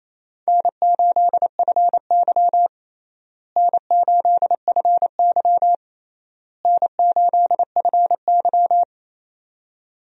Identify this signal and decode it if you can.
Morse code, heard as a keyed tone, about 28 wpm, 700 Hz. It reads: N8FY N8FY N8FY
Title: USA Call Signs - 4 Characters - Repeated 3x - 28wpm